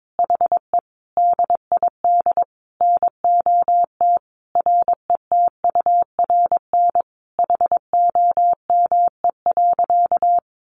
Morse code, heard as a keyed tone, about 22 wpm, 700 Hz. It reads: HE DID NOT RETURN HOME.
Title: Sentences from Top 800 Words 22wpm